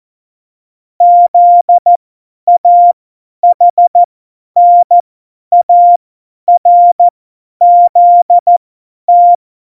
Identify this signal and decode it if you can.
Morse code, heard as a keyed tone, about 14 wpm, spaced as for 10 wpm, 700 Hz. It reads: ZAHNARZT
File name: Morsekurs / Witze / Witz 08. September 2020, 50 F70